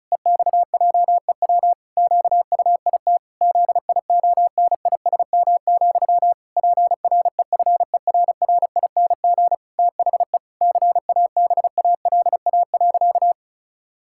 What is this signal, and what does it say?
Morse code, heard as a keyed tone, about 36 wpm, 700 Hz. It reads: EXJEW QUIT ZIONISM, PREFERRING THE CABALA.
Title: Pangram Challenge v2 - 35wpm